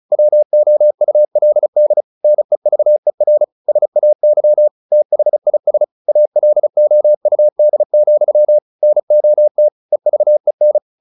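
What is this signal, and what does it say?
Morse code, heard as a keyed tone, about 34 wpm, 600 Hz. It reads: WOULD NEVER SAY THIS ALOUD, NOT EVEN
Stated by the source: Morse practice at 35 wpm from CW med Gnister, sending Ken Follett's novel Never in English.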